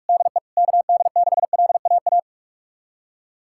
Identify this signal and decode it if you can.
Morse code, heard as a keyed tone, about 46 wpm, 700 Hz. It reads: DE KD6LAA
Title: USA Callsign DE Callsign - 6 Characters - 45wpm